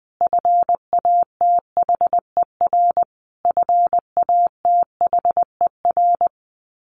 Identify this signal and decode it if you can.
Morse code, heard as a keyed tone, about 20 wpm, 700 Hz. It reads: FATHER FATHER
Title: Top 500 Words Repeated 20wpm